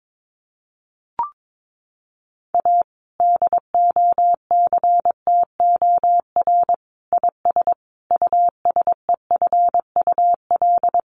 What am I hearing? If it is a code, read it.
Morse code, heard as a keyed tone, about 22 wpm, 700 Hz. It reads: A DOCTOR IS USEFUL
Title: Sentences from Top 1500 Words 22wpm